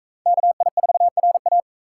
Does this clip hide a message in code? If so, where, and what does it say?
Yes: Morse code from 0.3 to 1.6 s, heard as a keyed tone, about 41 wpm, 700 Hz. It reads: KI4RA